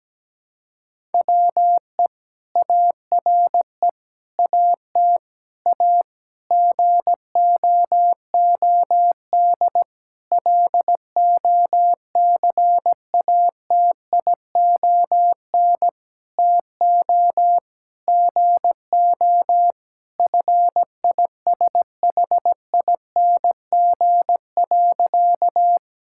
Morse code, heard as a keyed tone, about 17 wpm, 700 Hz. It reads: WE ARE AT A GOOD LOCATION TO GO FISHING.
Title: Sentences from Top 1000 Words - Review Part 6 - 17wpm